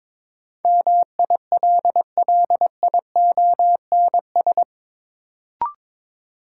Morse code, heard as a keyed tone, about 22 wpm, 700 Hz. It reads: MILLIONS
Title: Sets of 5 Words - New Word Drill - 22wpm